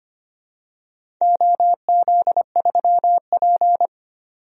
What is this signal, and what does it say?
Morse code, heard as a keyed tone, about 25 wpm, 700 Hz. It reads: OZ3P
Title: DXCC-100 Call Signs - 4 Characters - 25wpm